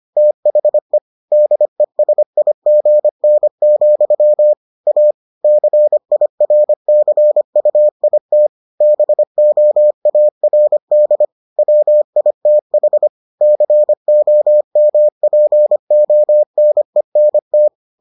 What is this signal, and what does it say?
Morse code, heard as a keyed tone, about 25 wpm, 600 Hz. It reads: THE DESIGN, A CIRCUIT BOARD WITH COMPONENT